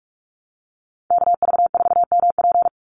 Morse code, heard as a keyed tone, about 44 wpm, 700 Hz. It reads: KV4MP
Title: USA Call Signs 45wpm